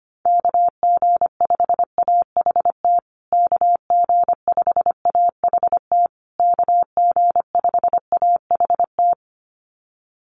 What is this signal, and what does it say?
Morse code, heard as a keyed tone, about 25 wpm, 700 Hz. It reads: KG5AHT KG5AHT KG5AHT
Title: USA Call Signs Repeated 3x 25wpm